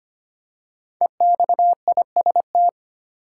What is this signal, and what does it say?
Morse code, heard as a keyed tone, about 25 wpm, 700 Hz. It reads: EXIST
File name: 5 Letter Words - First 3 Letters Given - 25wpm